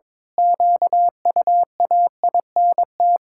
Morse code, heard as a keyed tone, about 22 wpm, 700 Hz. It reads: QUAINT
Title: Pangram Challenge v2 - 22wpm